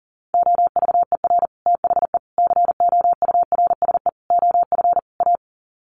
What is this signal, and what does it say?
Morse code, heard as a keyed tone, about 37 wpm, 700 Hz. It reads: OVER THE COURSE OF A
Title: Sets of 5 Words Encore 40wpm